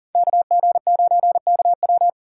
Morse code, heard as a keyed tone, about 40 wpm, 700 Hz. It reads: KG9KW
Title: USA Call Signs - 5 Characters - 8x Character Spacing - 40wpm